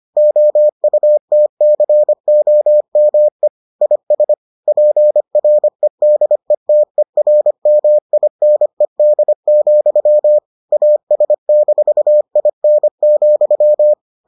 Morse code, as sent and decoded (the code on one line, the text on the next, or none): --- ..- - -.-. --- -- . / .. ... / .--. .-. -.. - . .-. -- .. -. -.. --..-- / .- ... -....- .. -. --..--
OUTCOME IS PREDETERMINED, AS-IN,